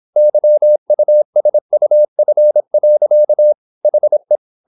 Morse code, heard as a keyed tone, about 26 wpm, 600 Hz. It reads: YUSUF. HE
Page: Never 26wpm | CW med Gnister